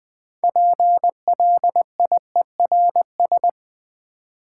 Morse code, heard as a keyed tone, about 20 wpm, 690 Hz. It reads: PLIERS
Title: Difficult Words with Hints - 20wpm